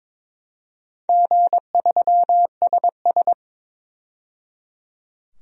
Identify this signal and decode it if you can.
Morse code, heard as a keyed tone, about 22 wpm, 700 Hz. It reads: G3SS
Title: DXCC-100 Call Signs - 4 Characters - 22wpm